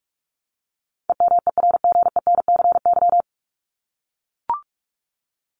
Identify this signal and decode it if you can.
Morse code, heard as a keyed tone, about 44 wpm, 700 Hz. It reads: EMERGENCY
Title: Top 1501-1750 Words 45wpm